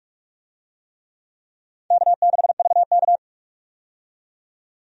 Morse code, heard as a keyed tone, about 44 wpm, 700 Hz. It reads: K6VK
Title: USA Call Signs - 4 Characters - 45wpm